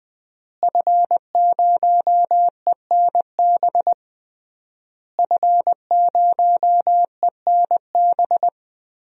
Morse code, heard as a keyed tone, about 20 wpm, 700 Hz. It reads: F0ENB F0ENB
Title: DXCC-100 Call Signs - 5 Characters - Repeated 3x - 20wpm